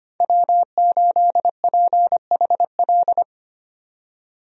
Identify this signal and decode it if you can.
Morse code, heard as a keyed tone, about 25 wpm, 700 Hz. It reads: W8PHL
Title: USA Call Signs - 5 Characters - 25wpm